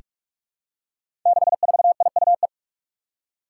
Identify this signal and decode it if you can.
Morse code, heard as a keyed tone, about 41 wpm, 700 Hz. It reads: 64IUE
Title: Five Challenging Letter-Number Combinations 45wpm